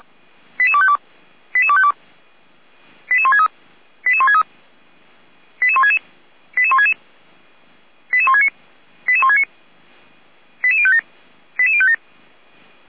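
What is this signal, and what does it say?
Five-tone selective call (ZVEI), heard as quick runs of five tones, tones of about 70 ms. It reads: 88252 88252 88163 88163 88166 88166 88169 88169 88057 88057
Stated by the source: ZVEI five-tone call, address 88191, received on an SDR.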